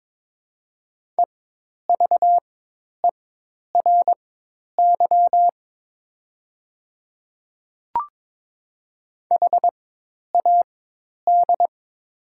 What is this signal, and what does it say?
Morse code, heard as a keyed tone, about 22 wpm, spaced as for 10 wpm, 700 Hz. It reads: EVERY HAD
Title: 4x Character Spacing - Top 200 Words - 22wpm